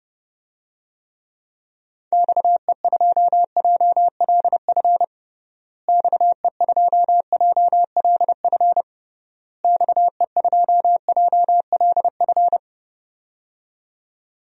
Morse code, heard as a keyed tone, about 32 wpm, 700 Hz. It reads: XE2JLF XE2JLF XE2JLF
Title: DXCC-100 Call Signs - 6 Characters - Repeated 3x - 30wpm